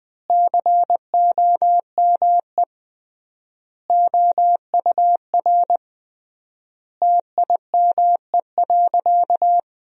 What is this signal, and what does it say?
Morse code, heard as a keyed tone, about 20 wpm, 700 Hz. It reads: COME OUR TIME.
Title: 3x Word-Spacing - Sentences from Top 200 Words - 20wpm